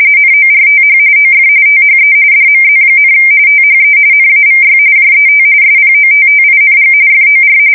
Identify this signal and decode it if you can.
RTTY, about 45.45 baud. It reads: NORTH OF THE DMZ, THE NAVY JET SPOTTING FOR
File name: fsk170 / battleshipnj